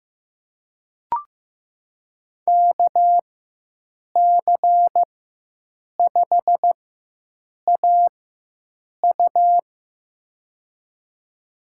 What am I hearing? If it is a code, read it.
Morse code, heard as a keyed tone, about 15 wpm, spaced as for 7 wpm, 700 Hz. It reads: KC5AU
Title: USA Call Signs - 5 Characters - 4x Character Spacing - 15wpm